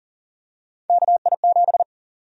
Morse code, heard as a keyed tone, about 40 wpm, 700 Hz. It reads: KI7
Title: USA Call Sign Prefixes - 40wpm